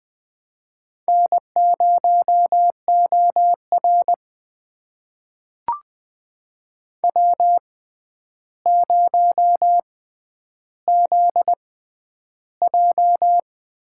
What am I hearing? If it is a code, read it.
Morse code, heard as a keyed tone, about 20 wpm, spaced as for 10 wpm, 700 Hz. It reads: N0OR W0ZJ
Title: USA Call Signs - 4 Characters - 6x Character Spacing - 20wpm